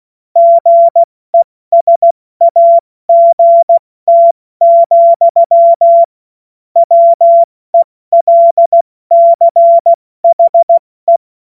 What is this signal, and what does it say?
Morse code, heard as a keyed tone, about 16 wpm, 680 Hz. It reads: GESAGT, WELCHE
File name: Morsekurs / Witze / Witz 01. August 2023 70 F80